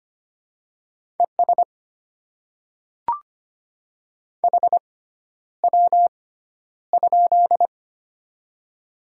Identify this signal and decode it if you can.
Morse code, heard as a keyed tone, about 25 wpm, spaced as for 11 wpm, 700 Hz. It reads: ES HW?
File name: Top 20 QSO Elements - 6x Character Spacing - 25wpm